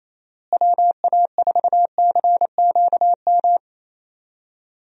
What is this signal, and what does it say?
Morse code, heard as a keyed tone, about 28 wpm, 700 Hz. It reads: WA4CQM
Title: USA Call Signs - 6 Characters - Repeated 2x - 28wpm